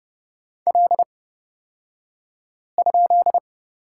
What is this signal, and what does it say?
Morse code, heard as a keyed tone, about 30 wpm, spaced as for 5 wpm, 700 Hz. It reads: L?
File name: Lesson 37d - Letter x - Two Characters 30wpm